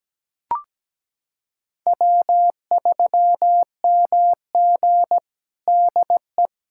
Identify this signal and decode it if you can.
Morse code, heard as a keyed tone, about 17 wpm, 700 Hz. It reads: W3MG DE
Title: USA Callsign DE Callsign - 4 Characters - 17wpm